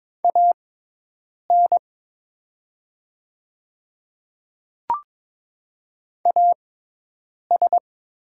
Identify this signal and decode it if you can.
Morse code, heard as a keyed tone, about 22 wpm, spaced as for 8 wpm, 700 Hz. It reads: AN AS
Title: 6x Character Spacing - Top 100 Words - 22wpm